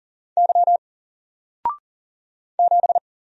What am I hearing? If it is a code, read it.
Morse code, heard as a keyed tone, about 40 wpm, 700 Hz. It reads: Y 7
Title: Single Letter-Number Alt 40wpm